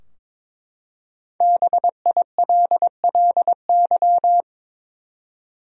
Morse code, heard as a keyed tone, about 22 wpm, 700 Hz. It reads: BILLY
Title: USA Common Names - Max 5 Letters - 22wpm